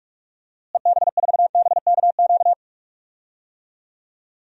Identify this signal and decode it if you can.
Morse code, heard as a keyed tone, about 46 wpm, 700 Hz. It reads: EB4BKQ
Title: DXCC-100 Call Signs - 6 Characters - Repeated 2x - 45wpm